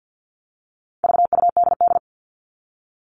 Morse code, heard as a keyed tone, about 48 wpm, 700 Hz. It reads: VUDD
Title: All Cut Numbers - 4-Digit 50wpm